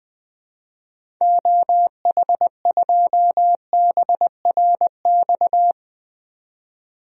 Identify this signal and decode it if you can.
Morse code, heard as a keyed tone, about 20 wpm, 700 Hz. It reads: OH2BRX
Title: DXCC-100 Call Signs - 6 Characters - 20wpm